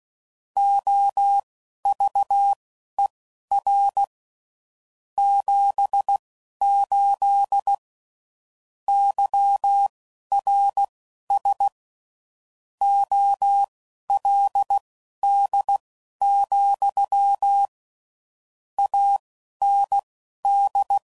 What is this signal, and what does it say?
Morse code, heard as a keyed tone, about 16 wpm, spaced as for 11 wpm, 800 Hz. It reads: OVER 78 YRS OLD, AND
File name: CW Code Groups @10WPM / QSOs / QSO kc7sac